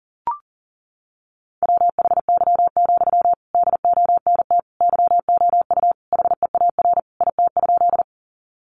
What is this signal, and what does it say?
Morse code, heard as a keyed tone, about 40 wpm, 700 Hz. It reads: WHY, DONT YOU HEAR IT?
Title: Sentences from Top 300 Words 40wpm